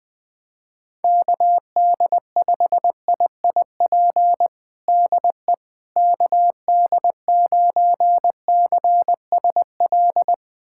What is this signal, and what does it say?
Morse code, heard as a keyed tone, about 20 wpm, 700 Hz. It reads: KD5IIP DE KD9CSL
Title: USA Callsign DE Callsign - 6 Characters - 20wpm